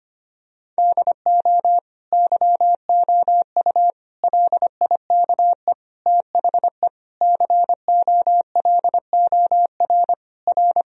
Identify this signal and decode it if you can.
Morse code, heard as a keyed tone, about 25 wpm, 700 Hz. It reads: DO YOU LIKE THE COLOR R
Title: Sentences from Top 1000 Words - Review Part 2 - 25wpm